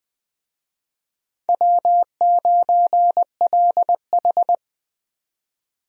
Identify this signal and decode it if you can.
Morse code, heard as a keyed tone, about 21 wpm, 700 Hz. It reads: W9LH
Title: USA Call Signs - 4 Characters - 20wpm